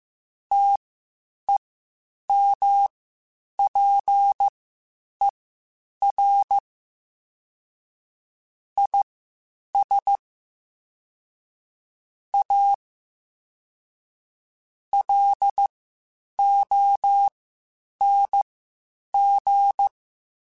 Morse code, heard as a keyed tone, about 15 wpm, spaced as for 8 wpm, 800 Hz. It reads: TEMPER IS A LONG
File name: Quotes Spaced / Quotes Spaced-mp3s / Temper sp @13